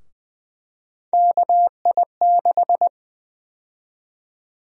Morse code, heard as a keyed tone, about 20 wpm, 700 Hz. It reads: KI6
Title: USA Call Sign Prefixes - Repeated 2x - 20wpm